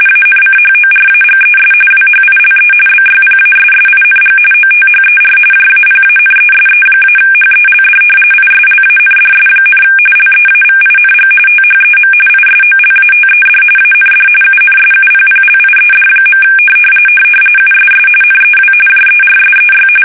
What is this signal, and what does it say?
RTTY, about 45.45 baud. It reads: ANCHORAGES WHILE BYPASSING OTHER JAPANESE-HELD ISLANDS, LEAVING THEM ISOLATED AND UNSUPPLIED. USS NEW JERSEY
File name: fsk850 / battleshipnj